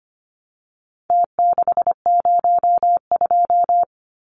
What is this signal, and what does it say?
Morse code, heard as a keyed tone, about 25 wpm, 700 Hz. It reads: T602
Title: Numbers - Contest Style - 4-Digit 25wpm